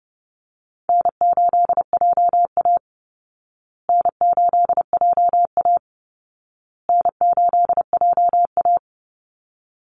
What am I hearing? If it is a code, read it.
Morse code, heard as a keyed tone, about 30 wpm, 700 Hz. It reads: N8JA N8JA N8JA
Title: Call Signs Repeated 2x - 30wpm